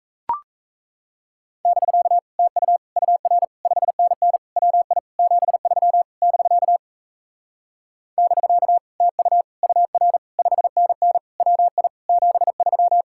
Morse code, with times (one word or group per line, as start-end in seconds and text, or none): 1.6-2.2 <BK>
2.4-2.8 TU
3.0-3.4 UR
3.6-4.4 5NN
4.6-5.0 WI
5.2-6.0 73
6.2-6.8 <BK>
8.2-8.8 <BK>
9.0-9.4 TU
9.6-10.2 UR
10.4-11.2 5NN
11.4-11.9 WI
12.1-13.0 73